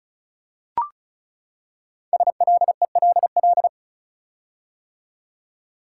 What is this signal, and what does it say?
Morse code, heard as a keyed tone, about 36 wpm, 700 Hz. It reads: SLELL
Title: Code Groups - 5 Letters - 35wpm